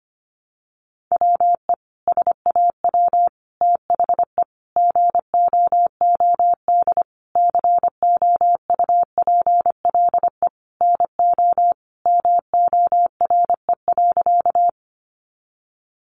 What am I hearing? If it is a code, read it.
Morse code, heard as a keyed tone, about 25 wpm, 700 Hz. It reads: WE SAW THE GOOD COUPLE NO MORE.
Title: Sentences from Top 1100 Words 25wpm